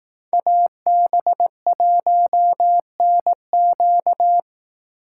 Morse code, heard as a keyed tone, about 18 wpm, 700 Hz. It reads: AB1NQ
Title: Speed Racing - USA 2x2 Call Signs 15wpm